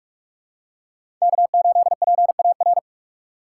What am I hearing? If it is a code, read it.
Morse code, heard as a keyed tone, about 43 wpm, 700 Hz. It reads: K8PAR
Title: USA Call Signs - 4x Character Spacing - 45wpm